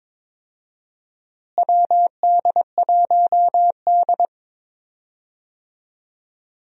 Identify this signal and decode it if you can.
Morse code, heard as a keyed tone, about 21 wpm, 700 Hz. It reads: WD1D
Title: USA Call Signs - 4 Characters - Repeated 2x - 22wpm